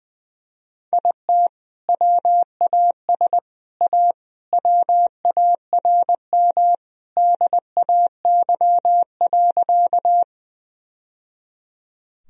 Morse code, heard as a keyed tone, about 21 wpm, 700 Hz. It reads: IT WAS A WARM DAY.
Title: Sentences from Top 1000 Words - Review Part 2 - 20wpm